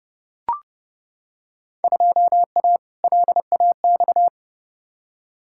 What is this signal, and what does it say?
Morse code, heard as a keyed tone, about 30 wpm, 700 Hz. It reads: 2A LAX
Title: Field Day Exchanges - 30wpm